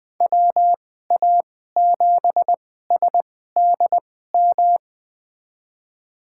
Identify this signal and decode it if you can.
Morse code, heard as a keyed tone, about 20 wpm, spaced as for 15 wpm, 700 Hz. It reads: WA7SDM
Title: USA Call Signs - 2x Character Spacing - 20wpm